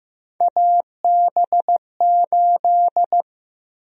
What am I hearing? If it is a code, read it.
Morse code, heard as a keyed tone, about 15 wpm, 700 Hz. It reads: AB8
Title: USA Call Sign Prefixes - 15wpm